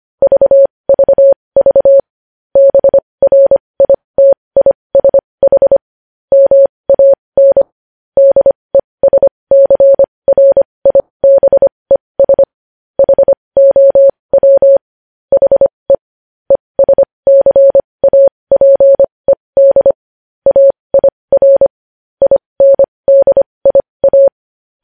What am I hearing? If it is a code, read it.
Morse code, heard as a keyed tone, about 25 wpm, spaced as for 20 wpm, 550 Hz. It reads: VVV BRITISH MAN DESCRIBES HOW HE ESCAPED AIR INDIA
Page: News Headlines in Morse Code at 20 WPM